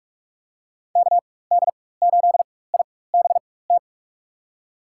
Morse code, heard as a keyed tone, about 46 wpm, spaced as for 21 wpm, 700 Hz. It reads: KD8IBT
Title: USA Call Signs - 6 Characters - 4x Character Spacing - 45wpm